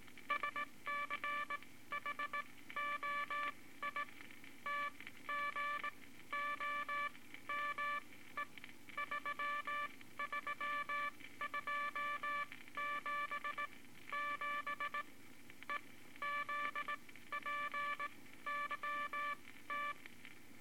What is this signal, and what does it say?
Morse code, heard as a keyed tone, about 19 wpm, spaced as for 14 wpm, 1160 Hz, with noise. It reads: SCHOI TGOME 33277 EZPYT
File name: Morse Code Records / 18 B LCM 165 Ten Words Per Minute